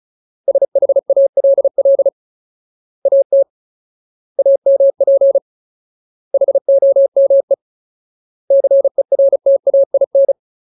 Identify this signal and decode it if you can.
Morse code, heard as a keyed tone, about 34 wpm, 550 Hz. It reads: SHALL AT AMP HOME CERTAIN
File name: Ultimate Word List / 35WPM Ultimate